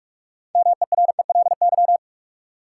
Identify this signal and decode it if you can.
Morse code, heard as a keyed tone, about 46 wpm, 700 Hz. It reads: MERELY
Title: Sentences from Top 1400 Words - New Word Drill - 45wpm